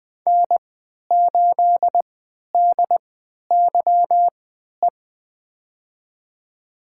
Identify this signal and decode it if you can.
Morse code, heard as a keyed tone, about 21 wpm, spaced as for 12 wpm, 700 Hz. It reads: N8DYE
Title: USA Call Signs - 5 Characters - 3x Character Spacing - 20wpm